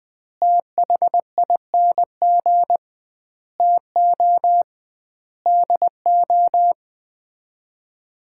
Morse code, heard as a keyed tone, about 20 wpm, 700 Hz. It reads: THING TO DO
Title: 2x Word-Spacing - Sets of 5 Words - 20wpm